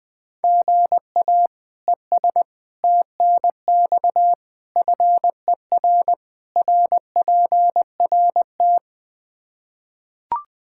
Morse code, heard as a keyed tone, about 20 wpm, 700 Hz. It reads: GA ES TNX FER RPRT
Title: Practice QSO - Exchange 1 - Beginning - 20wpm